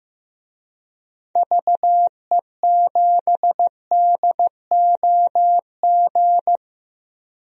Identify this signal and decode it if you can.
Morse code, heard as a keyed tone, about 15 wpm, 700 Hz. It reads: VE7DOG
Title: DXCC-100 Call Signs - 6 Characters - 15wpm